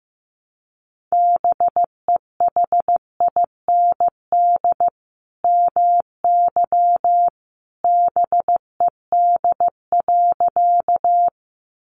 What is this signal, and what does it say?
Morse code, heard as a keyed tone, about 15 wpm, 700 Hz. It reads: BEHIND MY BED.
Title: Sentences from Top 500 Words 15wpm